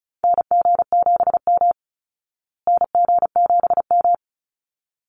Morse code, heard as a keyed tone, about 34 wpm, 700 Hz. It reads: NG7M NG7M
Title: Call Signs Repeated 3x - 35wpm